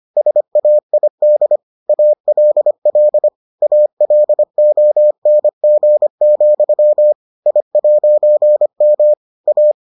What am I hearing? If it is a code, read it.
Morse code, heard as a keyed tone, about 25 wpm, 600 Hz. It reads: SAID ALL ALONG, I'M A